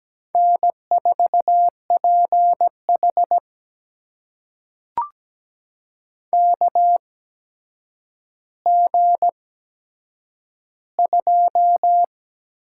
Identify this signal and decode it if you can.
Morse code, heard as a keyed tone, about 17 wpm, spaced as for 8 wpm, 700 Hz. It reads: N4PH KG2
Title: USA Call Signs - 4 Characters - 8x Character Spacing - 17wpm